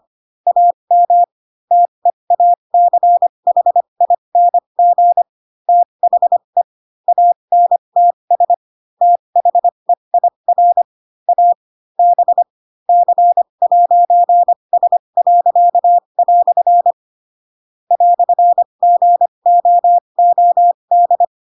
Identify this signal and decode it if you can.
Morse code, heard as a keyed tone, about 25 wpm, spaced as for 22 wpm, 700 Hz. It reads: AM TEACHING THE ANTS THEIR A B C'S." "GOOD
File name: Pinocchio / Pinocchio ch2 18 25 ik1hge